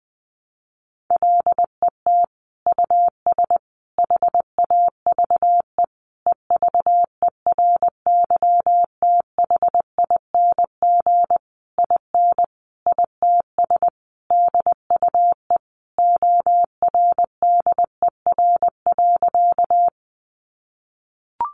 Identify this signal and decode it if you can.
Morse code, heard as a keyed tone, about 19 wpm, 700 Hz. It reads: LET US HAVE EVERYTHING IN ITS DUE ORDER.
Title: Sentences from Top 900 Words 20wpm